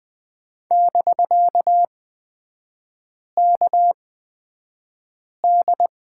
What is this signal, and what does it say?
Morse code, heard as a keyed tone, about 20 wpm, spaced as for 5 wpm, 700 Hz. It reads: <BK>KD
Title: 250 Lesson 40e - BK - Three Characters 20wpm